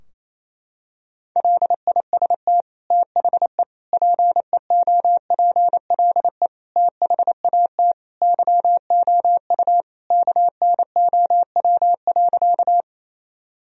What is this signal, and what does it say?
Morse code, heard as a keyed tone, about 28 wpm, 700 Hz. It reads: LIST THE PEOPLE THAT YOU KNOW.
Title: Sentences from Top 1000 Words - Review Part 2 - 28wpm